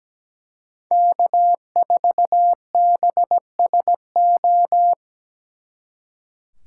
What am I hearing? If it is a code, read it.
Morse code, heard as a keyed tone, about 20 wpm, 700 Hz. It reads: K4BSO
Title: Speed Racing - USA 1x3 Call Signs 17wpm